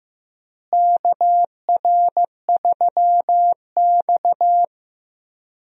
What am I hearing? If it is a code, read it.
Morse code, heard as a keyed tone, about 16 wpm, 700 Hz. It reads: KR3X
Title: USA Call Signs - 4 Characters - Repeated 2x - 15wpm